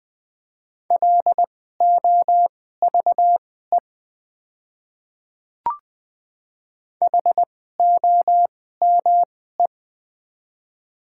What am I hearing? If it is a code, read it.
Morse code, heard as a keyed tone, about 20 wpm, spaced as for 14 wpm, 700 Hz. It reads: LOVE HOME
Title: 2x Character Spacing - Top 500 Words - 20wpm